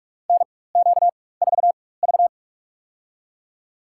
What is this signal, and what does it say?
Morse code, heard as a keyed tone, about 45 wpm, spaced as for 21 wpm, 700 Hz. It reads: NQ4V
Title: USA Call Signs - 4 Characters - 4x Character Spacing - 45wpm